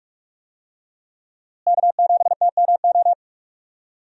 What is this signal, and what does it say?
Morse code, heard as a keyed tone, about 44 wpm, 700 Hz. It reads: K7TMO
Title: USA Call Signs - 5 Characters - Repeated 2x - 45wpm